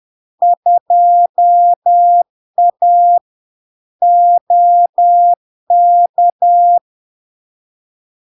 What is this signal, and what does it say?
Morse code, heard as a keyed tone, about 10 wpm, 700 Hz. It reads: 2A OK
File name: LARK-FD-2022-Log-in-CW-10wpm-700-Hz